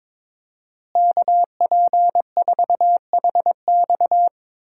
Morse code, heard as a keyed tone, about 22 wpm, 700 Hz. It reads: KP4HX
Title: USA Call Signs - 5 Characters - Repeated 3x - 22wpm